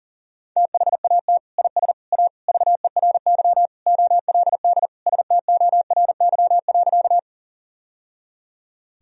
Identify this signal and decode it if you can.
Morse code, heard as a keyed tone, about 40 wpm, 700 Hz. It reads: THAT IS A VERY OLD STORY.
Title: Sentences from Top 1000 Words - Review Part 1 - 40wpm